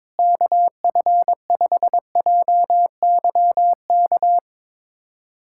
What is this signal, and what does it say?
Morse code, heard as a keyed tone, about 22 wpm, 700 Hz. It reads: KF5JYK
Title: USA Callsign DE Callsign 22wpm